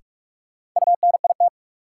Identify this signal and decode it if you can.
Morse code, heard as a keyed tone, about 46 wpm, 700 Hz. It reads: UNIT
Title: Sentences from Top 1000 Words - Review Part 2 - New Word Drill - 45wpm